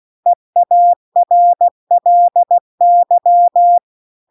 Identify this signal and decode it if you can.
Morse code, heard as a keyed tone, about 16 wpm, 700 Hz. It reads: EARLY